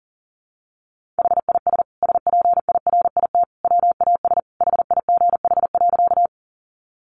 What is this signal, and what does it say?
Morse code, heard as a keyed tone, about 40 wpm, 700 Hz. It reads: HIS SPIRIT WAS HIGH.
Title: Sentences from Top 1400 Words 40wpm